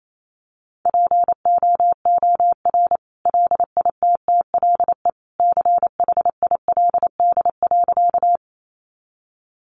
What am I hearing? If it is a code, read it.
Morse code, heard as a keyed tone, about 28 wpm, 700 Hz. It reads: POOR LITTLE CHILD.
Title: Sentences from Top 1000 Words 28wpm